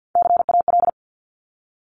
Morse code, heard as a keyed tone, about 50 wpm, 700 Hz. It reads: CAL
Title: Top Three Letter Combinations in Words 50wpm